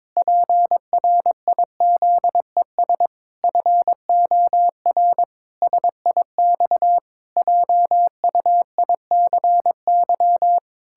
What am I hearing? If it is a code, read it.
Morse code, heard as a keyed tone, about 22 wpm, 700 Hz. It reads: PRIZES FOR SIX JUICY
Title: Pangram Challenge v2 - 22wpm